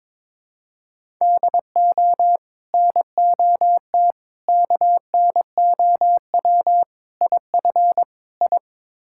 Morse code, heard as a keyed tone, about 22 wpm, 700 Hz. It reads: DO NOT KNOW IF I
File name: practice / 22wpm / Build A Phrase - 5-Word Phrases - 22wpm